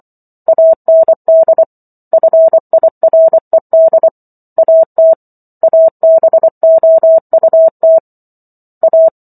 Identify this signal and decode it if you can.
Morse code, heard as a keyed tone, about 24 wpm, 650 Hz. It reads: AND FIRED AT ABOUT A